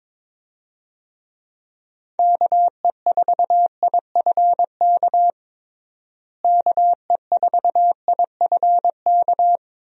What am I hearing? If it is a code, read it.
Morse code, heard as a keyed tone, about 22 wpm, 700 Hz. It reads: KE4IFK KE4IFK
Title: USA Call Signs - 6 Characters - Repeated 3x - 22wpm